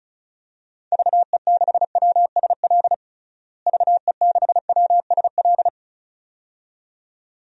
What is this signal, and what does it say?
Morse code, heard as a keyed tone, about 34 wpm, 700 Hz. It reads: VE6WSL VE6WSL
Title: DXCC-100 Call Signs - 6 Characters - Repeated 2x - 35wpm